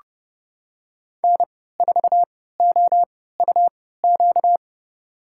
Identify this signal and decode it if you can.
Morse code, heard as a keyed tone, about 30 wpm, spaced as for 17 wpm, 700 Hz. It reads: N4OUQ
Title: USA Call Signs - 3x Character Spacing - 30wpm